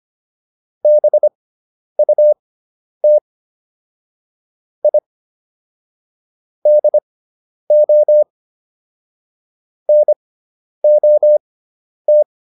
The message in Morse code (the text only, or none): BUT I DO NOT